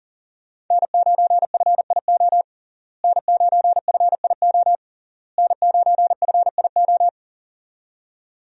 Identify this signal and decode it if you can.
Morse code, heard as a keyed tone, about 37 wpm, 700 Hz. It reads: N9FIO N9FIO N9FIO
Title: USA Call Signs - 5 Characters - Repeated 3x - 40wpm